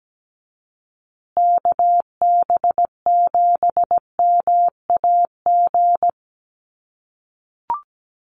Morse code, heard as a keyed tone, about 17 wpm, 700 Hz. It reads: KB7MAG
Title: USA Call Signs Repeated 3x 17wpm